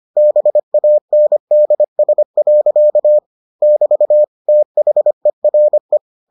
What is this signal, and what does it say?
Morse code, heard as a keyed tone, about 25 wpm, 600 Hz. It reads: BANDS. <BT> THERE